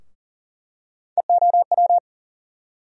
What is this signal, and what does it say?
Morse code, heard as a keyed tone, about 40 wpm, 700 Hz. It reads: EOW
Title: USA Call Sign Postfixes - 3 Letters 40wpm